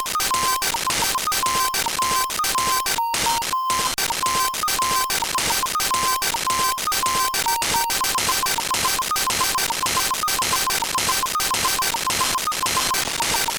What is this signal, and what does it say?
There is no code.